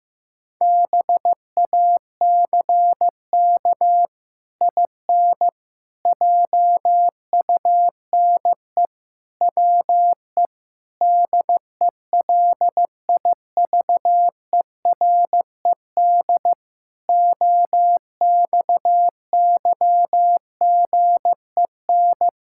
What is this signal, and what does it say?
Morse code, heard as a keyed tone, about 16 wpm, 700 Hz. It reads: BACK IN JUNE WE DELIVERED OXYGEN
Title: Pangram Challenge v2 - 15wpm